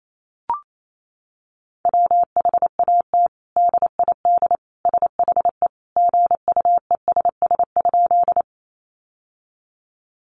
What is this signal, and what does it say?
Morse code, heard as a keyed tone, about 28 wpm, 700 Hz. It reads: WHAT DID SHE GUESS?
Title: Sentences from Top 1500 Words 28wpm